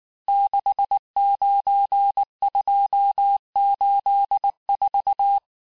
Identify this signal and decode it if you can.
Morse code, heard as a keyed tone, about 19 wpm, 800 Hz. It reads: 69284
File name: cours-du-mercredi / audios / cours-869-texte-CHIFFRE-vitesse-18-MPM